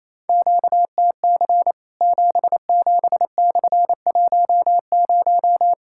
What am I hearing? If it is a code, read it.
Morse code, heard as a keyed tone, about 28 wpm, 700 Hz. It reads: QTC 77/10
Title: Worked All Europe DX Contest - QTC Exchange - 28wpm